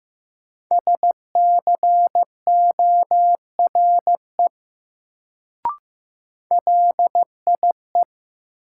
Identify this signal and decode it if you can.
Morse code, heard as a keyed tone, about 15 wpm, 700 Hz. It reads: SCORE LIE
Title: Sentences from Top 1600 Words - New Word Drill Alt - 15wpm